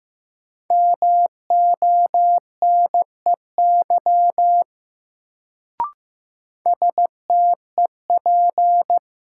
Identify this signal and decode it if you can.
Morse code, heard as a keyed tone, about 15 wpm, 700 Hz. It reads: MONEY STEP
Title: Sentences from Top 400 Words - New Word Drill - 15wpm